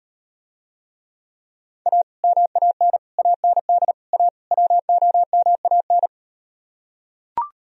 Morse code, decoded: A MAN AND A WOMAN

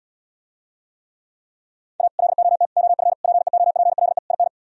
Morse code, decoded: A LOT OF PEOPLE IN

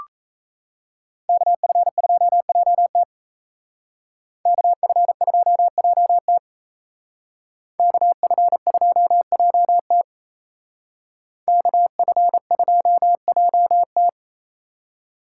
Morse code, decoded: KF2JT KF2JT KF2JT KF2JT

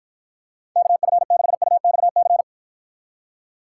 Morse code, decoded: KF6UXC

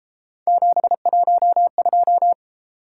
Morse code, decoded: 712

